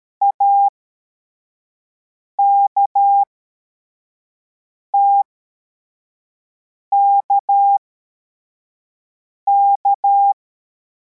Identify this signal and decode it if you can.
Morse code, heard as a keyed tone, about 13 wpm, spaced as for 4 wpm, 800 Hz. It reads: AKTKK